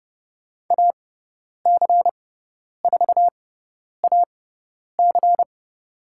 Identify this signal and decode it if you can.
Morse code, heard as a keyed tone, about 30 wpm, spaced as for 10 wpm, 700 Hz. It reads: AC4AC
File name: Lesson 18g - Letter m - Call Signs 30wpm